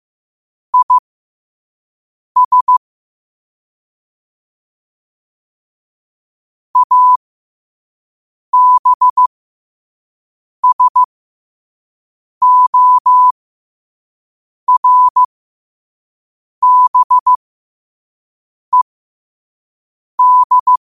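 Morse code, heard as a keyed tone, about 15 wpm, spaced as for 5 wpm, 1000 Hz. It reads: IS ABSORBED
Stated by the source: Morse at 5 wpm.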